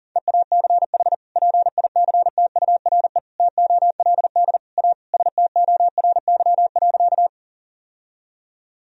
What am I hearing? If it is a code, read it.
Morse code, heard as a keyed tone, about 40 wpm, 700 Hz. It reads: EACH PICTURE TOLD A STORY.
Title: Sentences from Top 1000 Words - Review Part 1 - 40wpm